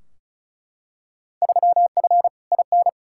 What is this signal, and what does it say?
Morse code, heard as a keyed tone, about 36 wpm, 700 Hz. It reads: 3F IN
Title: Field Day Exchanges - 35wpm